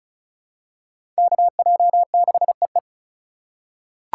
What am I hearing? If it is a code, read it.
Morse code, heard as a keyed tone, about 36 wpm, 700 Hz. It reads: KJ6EE
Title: USA Call Signs - 5 Characters - 2x Character Spacing - 35wpm